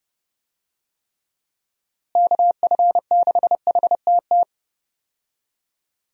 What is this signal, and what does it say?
Morse code, heard as a keyed tone, about 30 wpm, 700 Hz. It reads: KF6HTT
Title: USA Call Signs - 6 Characters - Repeated 3x - 30wpm